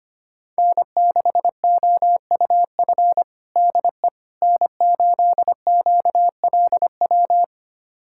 Morse code, heard as a keyed tone, about 25 wpm, 700 Hz. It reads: N6OUF DE N8QLW
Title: USA Callsign DE Callsign - 5 Characters - 25wpm